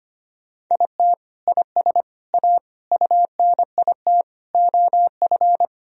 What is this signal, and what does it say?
Morse code, heard as a keyed tone, about 25 wpm, 700 Hz. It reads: IT IS A UNIT OF
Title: Sentences from Top 1000 Words - Review Part 2 - 25wpm